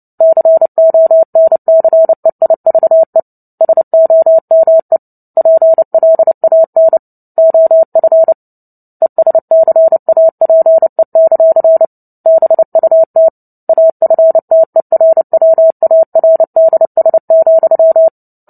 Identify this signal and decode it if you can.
Morse code, heard as a keyed tone, about 30 wpm, 650 Hz. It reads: CONCEIVE SOME PLAN OF ESCAPE; BUT AFTERWARDS,
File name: Book Two / 03-Chapter 3 - 29 WPM